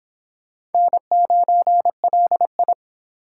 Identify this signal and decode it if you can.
Morse code, heard as a keyed tone, about 29 wpm, 700 Hz. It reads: N9LI